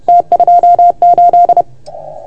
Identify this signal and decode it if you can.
Morse code, heard as a keyed tone, about 30 wpm, 690 Hz. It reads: T28
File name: Archiv / CUC381 / OK1FGD CUC 381